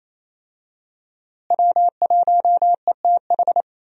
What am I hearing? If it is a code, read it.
Morse code, heard as a keyed tone, about 28 wpm, 700 Hz. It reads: W1ETH
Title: USA Call Signs - 5 Characters - Repeated 2x - 28wpm